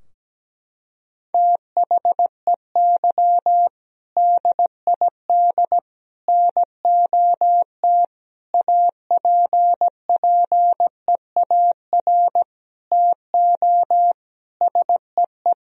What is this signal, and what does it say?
Morse code, heard as a keyed tone, about 17 wpm, 700 Hz. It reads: THEY DID NOT APPEAR TO SEE
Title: Sentences from Top 1000 Words - Review Part 2 - 17wpm